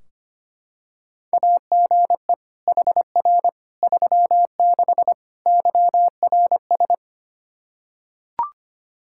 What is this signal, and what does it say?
Morse code, heard as a keyed tone, about 25 wpm, 700 Hz. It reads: AGE HR 36 YRS
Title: Practice QSO - Exchange 3 - More Info - 25wpm